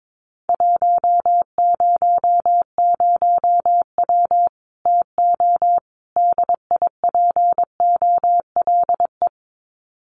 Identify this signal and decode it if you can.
Morse code, heard as a keyed tone, about 22 wpm, 700 Hz. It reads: 100W TO DIPOLE
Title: Common QSO Phrases 22wpm